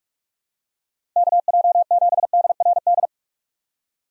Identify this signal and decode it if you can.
Morse code, heard as a keyed tone, about 46 wpm, 700 Hz. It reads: KJ7DRD
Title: USA Call Signs - 6 Characters - Repeated 2x - 45wpm